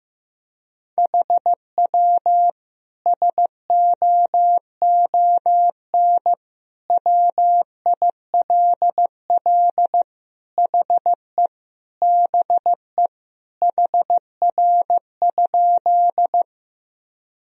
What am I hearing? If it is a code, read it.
Morse code, heard as a keyed tone, about 15 wpm, 700 Hz. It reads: HW SOON WILL HE BE HR?
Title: Sentences from Top 400 Words and Abbreviations 15wpm